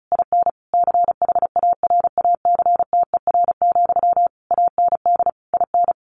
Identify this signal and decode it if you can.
Morse code, heard as a keyed tone, about 34 wpm, 700 Hz. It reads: IN CHARACTER, AND IN
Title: Sentences from Top 1400 Words 35wpm